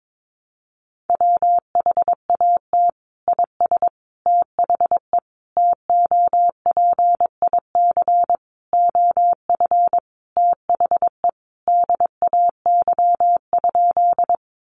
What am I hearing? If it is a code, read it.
Morse code, heard as a keyed tone, about 22 wpm, 700 Hz. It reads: WHAT IS THE TOPIC OF THE DAY?
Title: Sentences from Top 1200 Words 22wpm